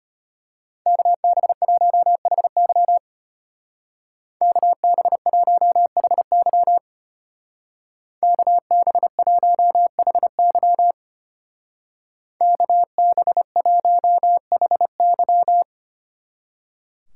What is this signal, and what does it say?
Morse code, heard as a keyed tone, about 30 wpm, 700 Hz. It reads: KB1HY KB1HY KB1HY KB1HY
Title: Speed Racing - USA 2x2 Call Signs 25wpm